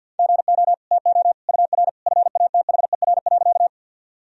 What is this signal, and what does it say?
Morse code, heard as a keyed tone, about 52 wpm, 700 Hz. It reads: GO TO UR FATHER.
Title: Sentences from Top 300 Words and Abbreviations 50wpm